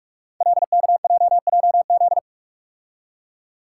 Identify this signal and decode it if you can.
Morse code, heard as a keyed tone, about 44 wpm, 700 Hz. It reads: LKJJZ
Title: Code Groups - 5 Letters - 45wpm